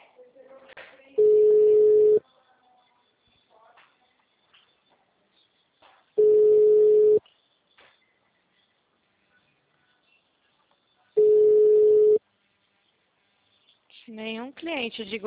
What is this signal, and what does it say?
Ringback tone (Europe, 425 Hz), with speech.
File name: out- -2039- - - .62262